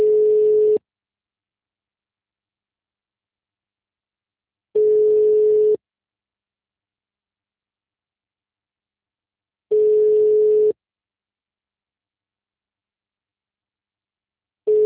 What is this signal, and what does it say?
Ringback tone (Europe, 425 Hz).
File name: out- -2016- - - .31